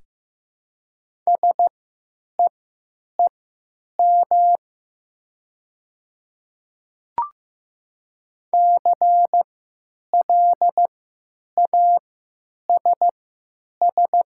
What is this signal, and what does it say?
Morse code, heard as a keyed tone, about 15 wpm, spaced as for 9 wpm, 700 Hz. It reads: SEEM CLASS
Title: 3x Character Spacing - Top 500 Words - 15wpm